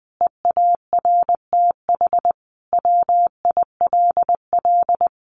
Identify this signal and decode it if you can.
Morse code, heard as a keyed tone, about 20 wpm, 700 Hz. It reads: EARTH WILL
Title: Sentences from Top 1100 Words 20wpm